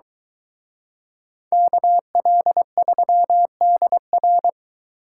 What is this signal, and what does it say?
Morse code, heard as a keyed tone, about 22 wpm, 700 Hz. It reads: KL3DR